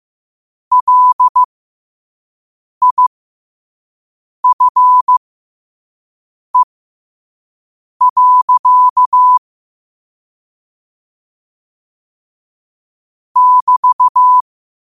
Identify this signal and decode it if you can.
Morse code, heard as a keyed tone, about 15 wpm, spaced as for 5 wpm, 1000 Hz. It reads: LIFE. <BT>
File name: qotdfiles / 5wpm